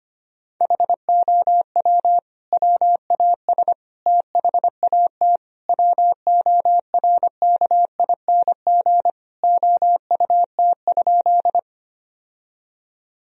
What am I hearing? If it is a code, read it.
Morse code, heard as a keyed tone, about 25 wpm, 700 Hz. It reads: HOW WAS THAT WORKING OUT?
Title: Sentences from Top 1000 Words - Review Part 4 - 25wpm